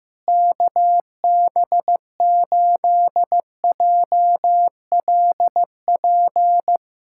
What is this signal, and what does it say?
Morse code, heard as a keyed tone, about 15 wpm, 700 Hz. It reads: KB8JLP
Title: USA Callsign DE Callsign - 6 Characters - 15wpm